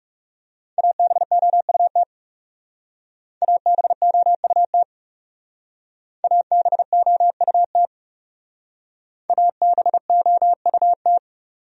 Morse code, heard as a keyed tone, about 37 wpm, 700 Hz. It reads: ABOUT ABOUT ABOUT ABOUT
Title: Speed-Racing Top 100 Words 30wpm